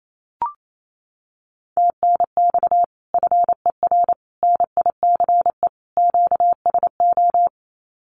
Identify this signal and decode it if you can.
Morse code, heard as a keyed tone, about 28 wpm, 700 Hz. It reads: TNX FER NICE QSO
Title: Common QSO Phrases Encore 28wpm